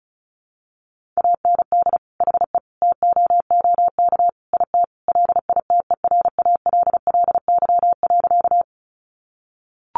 Morse code, heard as a keyed tone, about 34 wpm, 700 Hz. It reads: AND HE TOOK IT LITERALLY.
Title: Sentences from Top 1400 Words 35wpm